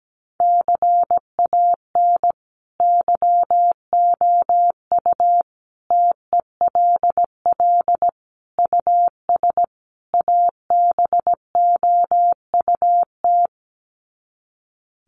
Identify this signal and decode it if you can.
Morse code, heard as a keyed tone, about 17 wpm, 700 Hz. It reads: CAN YOU TELL US ABOUT
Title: Sets of 5 Words Encore2 17wpm